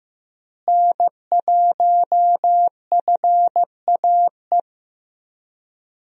Morse code, heard as a keyed tone, about 15 wpm, 700 Hz. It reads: N1FAE
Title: USA Call Signs - 5 Characters - Repeated 3x - 15wpm